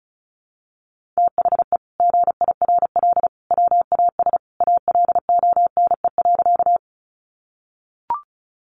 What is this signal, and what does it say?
Morse code, heard as a keyed tone, about 34 wpm, 700 Hz. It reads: THE GIRL WAS ALONE.
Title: Sentences from Top 1000 Words 35wpm